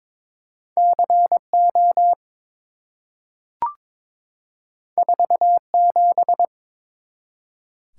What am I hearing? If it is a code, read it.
Morse code, heard as a keyed tone, about 23 wpm, 700 Hz. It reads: CO 47
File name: practice / 22wpm / Two Letter-Number Combinations - 22wpm